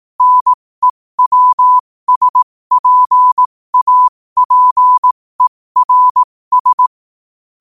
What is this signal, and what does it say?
Morse code, heard as a keyed tone, about 18 wpm, spaced as for 15 wpm, 1000 Hz. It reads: NEWSPAPERS
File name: qotdfiles / 15wpm